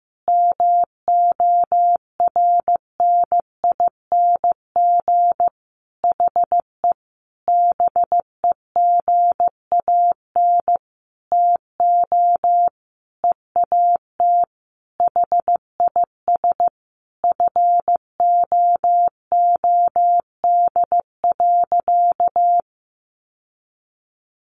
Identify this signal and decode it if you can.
Morse code, heard as a keyed tone, about 15 wpm, 700 Hz. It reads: MORNING HE BEGAN TO EAT HIS FOOD.
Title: Sentences from Top 1100 Words 15wpm